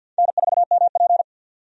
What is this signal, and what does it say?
Morse code, heard as a keyed tone, about 52 wpm, 700 Hz. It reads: N4MP
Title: USA Call Signs - 4 Characters - 50wpm